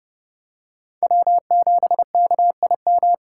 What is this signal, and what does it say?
Morse code, heard as a keyed tone, about 31 wpm, 700 Hz. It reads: W7KIM
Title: USA Call Signs - 5 Characters - 6x Character Spacing - 30wpm